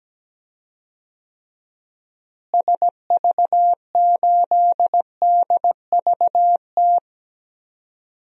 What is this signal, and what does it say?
Morse code, heard as a keyed tone, about 17 wpm, 700 Hz. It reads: SV8DVT
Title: DXCC-100 Call Signs - 6 Characters - Repeated 3x - 17wpm